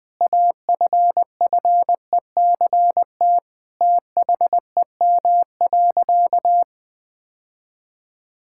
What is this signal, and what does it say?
Morse code, heard as a keyed tone, about 20 wpm, 700 Hz. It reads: AFFECT THEM.
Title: Sentences from Top 1000 Words - Review Part 6 - 20wpm